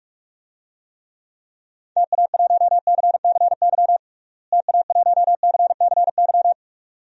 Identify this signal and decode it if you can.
Morse code, heard as a keyed tone, about 46 wpm, 700 Hz. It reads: TA1CCY TA1CCY
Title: DXCC-100 Call Signs - 6 Characters - Repeated 2x - 45wpm